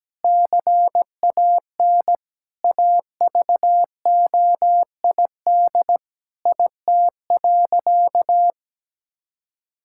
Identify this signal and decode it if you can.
Morse code, heard as a keyed tone, about 17 wpm, 700 Hz. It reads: CAN AVOID IT.
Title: Sentences from Top 1000 Words - Review Part 6 - 17wpm